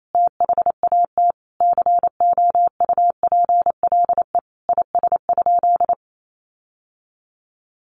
Morse code, heard as a keyed tone, about 28 wpm, 700 Hz. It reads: THAT COUPLE IS?
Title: Sentences from Top 1100 Words 28wpm